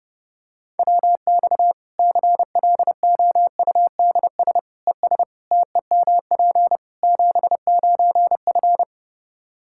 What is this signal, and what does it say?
Morse code, heard as a keyed tone, about 30 wpm, 700 Hz. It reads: WX CLOUDS ES TEMP 79F
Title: Practice QSO - Exchange 2 - Info - 30wpm